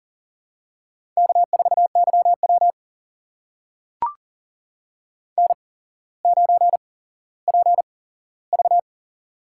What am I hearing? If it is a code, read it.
Morse code, heard as a keyed tone, about 44 wpm, spaced as for 18 wpm, 700 Hz. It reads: K4YW N9PV
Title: USA Call Signs - 4 Characters - 8x Character Spacing - 40wpm